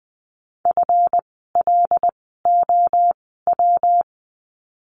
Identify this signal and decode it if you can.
Morse code, heard as a keyed tone, about 20 wpm, spaced as for 14 wpm, 700 Hz. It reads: FLOW
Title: 139 Lesson 23f - Letter f - Words 20wpm